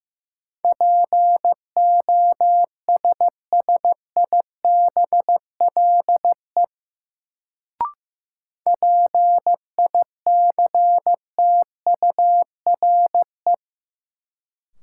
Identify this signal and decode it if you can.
Morse code, heard as a keyed tone, about 15 wpm, 700 Hz. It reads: POSSIBLE PICTURE
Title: Sentences from Top 500 Words - New Word Drill - 15wpm